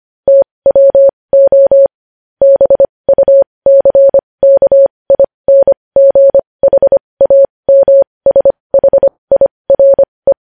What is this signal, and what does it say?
Morse code, heard as a keyed tone, about 25 wpm, spaced as for 20 wpm, 550 Hz. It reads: TWO BUCKINGHAMSHIRE
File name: podcast / News headlines-1x-25x20